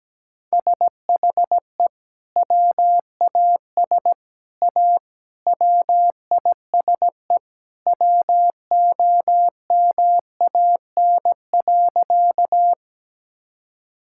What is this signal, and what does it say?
Morse code, heard as a keyed tone, about 17 wpm, 700 Hz. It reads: SHE WAS A WISE WOMAN.
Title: Sentences from Top 1600 Words 17wpm